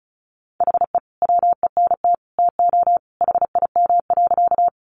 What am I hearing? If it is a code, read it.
Morse code, heard as a keyed tone, about 34 wpm, 700 Hz. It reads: HE WENT TO HIM.
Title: Sentences from Top 200 Words 35wpm